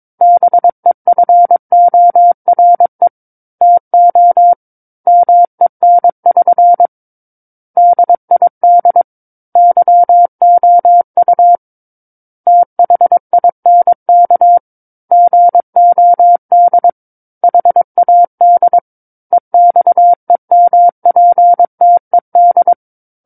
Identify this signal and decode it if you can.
Morse code, heard as a keyed tone, about 22 wpm, 700 Hz. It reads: BEFORE TO MEN<SN> DID YOU THINK GOD HAD EXEMPTED
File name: Book One / 13-Chapter 13 - 22 WPM